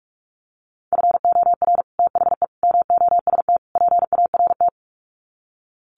Morse code, heard as a keyed tone, about 44 wpm, 700 Hz. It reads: FOR THE MOST PART
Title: Sets of 4 Words 45wpm